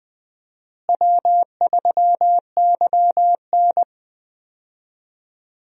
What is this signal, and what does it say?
Morse code, heard as a keyed tone, about 21 wpm, 700 Hz. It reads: W3YN
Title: USA Call Signs - 4 Characters - Repeated 3x - 20wpm